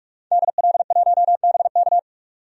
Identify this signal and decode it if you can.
Morse code, heard as a keyed tone, about 46 wpm, 700 Hz. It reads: DL1BK
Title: DXCC-100 Call Signs - 5 Characters - 45wpm